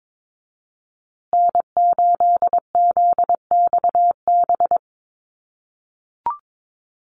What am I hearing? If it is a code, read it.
Morse code, heard as a keyed tone, about 22 wpm, 700 Hz. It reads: N8ZXB
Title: USA Call Signs Repeated 3x 22wpm